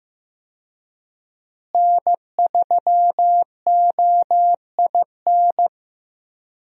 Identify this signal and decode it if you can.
Morse code, heard as a keyed tone, about 15 wpm, 700 Hz. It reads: N3OIN
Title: USA Call Signs - 5 Characters - Repeated 3x - 15wpm